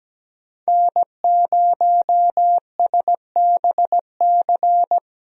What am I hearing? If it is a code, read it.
Morse code, heard as a keyed tone, about 17 wpm, 700 Hz. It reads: N0SBC
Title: USA Call Signs - 5 Characters - 17wpm